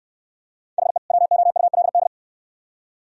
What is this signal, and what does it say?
Morse code, heard as a keyed tone, about 62 wpm, 700 Hz. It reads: HE WOULD